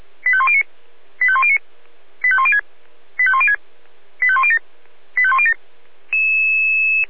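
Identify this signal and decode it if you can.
Five-tone selective call (ZVEI), heard as quick runs of five tones, tones of about 70 ms. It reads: 85109 85109 85106 85106 85107 85107
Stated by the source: ZVEI five-tone call 85105 received by SDR.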